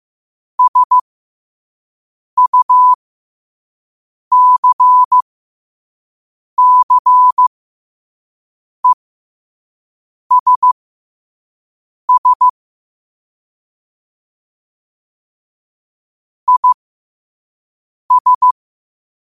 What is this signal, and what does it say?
Morse code, heard as a keyed tone, about 15 wpm, spaced as for 5 wpm, 1000 Hz. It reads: SUCCESS IS
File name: qotdfiles / 5wpm